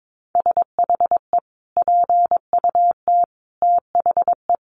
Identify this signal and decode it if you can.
Morse code, heard as a keyed tone, about 22 wpm, 700 Hz. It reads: SHE PUT THE
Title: Sentences from Top 1400 Words 22wpm